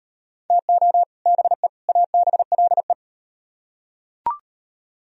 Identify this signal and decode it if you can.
Morse code, heard as a keyed tone, about 38 wpm, 700 Hz. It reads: TO BE ABLE